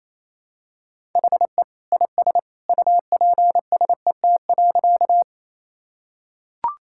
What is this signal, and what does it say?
Morse code, heard as a keyed tone, about 28 wpm, 700 Hz. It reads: HE IS UPSET.
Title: Sentences from Top 1000 Words - Review Part 5 - 28wpm